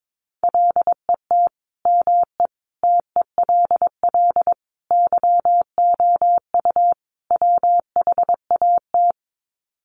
Morse code, heard as a keyed tone, about 22 wpm, 700 Hz. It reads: LET ME TELL YOU WHAT
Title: Sets of 5 Words Encore 22wpm